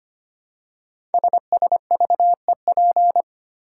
Morse code, heard as a keyed tone, about 25 wpm, 700 Hz. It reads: SSVEP
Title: Code Groups - 5 Letters - 25wpm